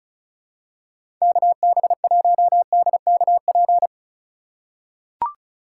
Morse code, heard as a keyed tone, about 36 wpm, 700 Hz. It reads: KB1DKP
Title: USA Call Signs - 6 Characters - 2x Character Spacing - 35wpm